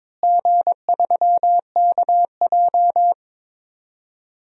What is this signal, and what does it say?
Morse code, heard as a keyed tone, about 22 wpm, 700 Hz. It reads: G3KJ
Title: DXCC-100 Call Signs - 4 Characters - Repeated 2x - 22wpm